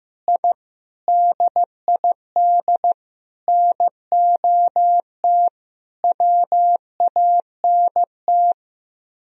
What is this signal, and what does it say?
Morse code, heard as a keyed tone, about 15 wpm, 700 Hz. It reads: I DID NOT WANT
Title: Speed-Racing - Sets of 5 Words 15wpm